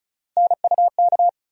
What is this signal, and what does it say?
Morse code, heard as a keyed tone, about 34 wpm, 700 Hz. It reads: NUK
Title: USA Call Sign Postfixes - 3 Letters 35wpm